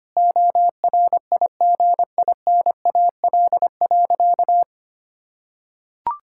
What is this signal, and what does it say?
Morse code, heard as a keyed tone, about 25 wpm, 700 Hz. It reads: ORIGINAL.
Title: Sentences from Top 1000 Words - Review Part 6 - 25wpm